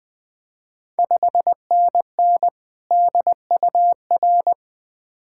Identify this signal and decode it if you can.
Morse code, heard as a keyed tone, about 20 wpm, 700 Hz. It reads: 5NN DUR
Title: Ontario QSO Party Exchange 20wpm